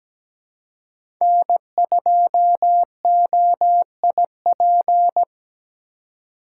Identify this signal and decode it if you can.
Morse code, heard as a keyed tone, about 17 wpm, 700 Hz. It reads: N2OIP
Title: USA Call Signs - 5 Characters - 2x Character Spacing - 17wpm